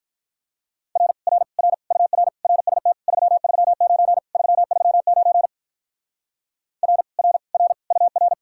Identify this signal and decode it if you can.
Morse code, heard as a keyed tone, about 45 wpm, 700 Hz. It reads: R R R UR RST 339 339 R R R UR